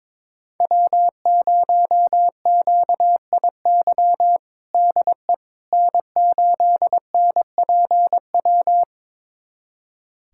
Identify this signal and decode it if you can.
Morse code, heard as a keyed tone, about 22 wpm, 700 Hz. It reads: W0QIY DE N8NPW
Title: USA Callsign DE Callsign - 5 Characters - 22wpm